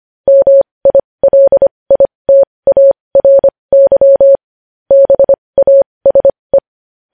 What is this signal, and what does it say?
Morse code, heard as a keyed tone, about 25 wpm, spaced as for 20 wpm, 550 Hz. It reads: MILITARY BASE
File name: podcast / News headlines-1x-25x20